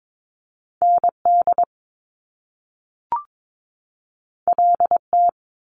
Morse code, heard as a keyed tone, about 22 wpm, 700 Hz. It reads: ND LT
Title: Top Two Letter Combinations in Words 22wpm